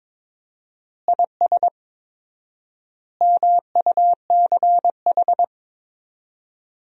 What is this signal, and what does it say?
Morse code, heard as a keyed tone, about 21 wpm, 700 Hz. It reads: IS MUCH
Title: 4x Word-Spacing - Sentences from Top 400 Words - 22wpm